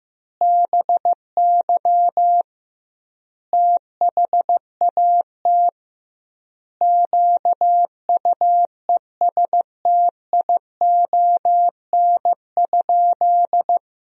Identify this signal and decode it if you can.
Morse code, heard as a keyed tone, about 16 wpm, 700 Hz. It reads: BY THAT QUESTION?
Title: 2x Word-Spacing - Sentences from Top 400 Words - 15wpm